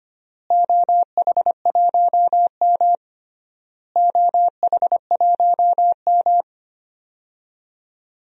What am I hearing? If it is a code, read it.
Morse code, heard as a keyed tone, about 25 wpm, 700 Hz. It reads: OH1M OH1M
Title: DXCC-100 Call Signs - 4 Characters - Repeated 3x - 25wpm